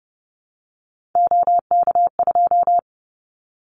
Morse code, heard as a keyed tone, about 30 wpm, 700 Hz. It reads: OK2
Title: Callsign Prefixes for Top 100 DXCC - 30wpm